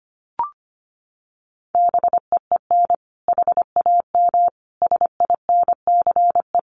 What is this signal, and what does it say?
Morse code, heard as a keyed tone, about 25 wpm, 700 Hz. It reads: BEEN HAM SINCE
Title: Common QSO Phrases 25wpm